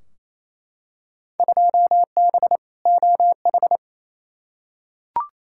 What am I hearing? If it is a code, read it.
Morse code, heard as a keyed tone, about 28 wpm, 700 Hz. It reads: 2B OH
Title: Field Day Exchanges Encore - 28wpm